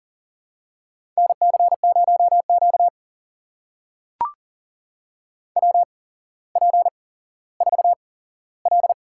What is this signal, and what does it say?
Morse code, heard as a keyed tone, about 40 wpm, spaced as for 17 wpm, 700 Hz. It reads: NC0Q WP4L
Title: USA Call Signs - 4 Characters - 8x Character Spacing - 40wpm